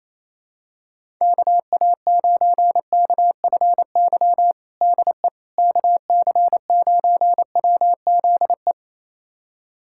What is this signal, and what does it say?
Morse code, heard as a keyed tone, about 28 wpm, 700 Hz. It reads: KA9KFY DE KC9WZE
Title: USA Callsign DE Callsign - 6 Characters - 28wpm